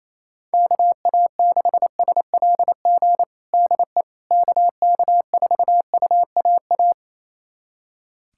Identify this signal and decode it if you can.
Morse code, heard as a keyed tone, about 28 wpm, 700 Hz. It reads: KA6SLG DE KK4UAA
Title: USA Callsign DE Callsign - 6 Characters - 28wpm